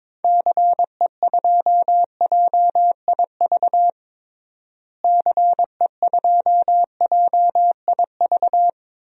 Morse code, heard as a keyed tone, about 22 wpm, 700 Hz. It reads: CE2JIV CE2JIV
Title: DXCC-100 Call Signs - 6 Characters - Repeated 3x - 22wpm